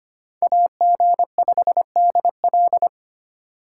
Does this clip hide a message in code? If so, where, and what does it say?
Yes: Morse code from 0.4 to 2.9 s, heard as a keyed tone, about 33 wpm, 700 Hz. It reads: AG5DL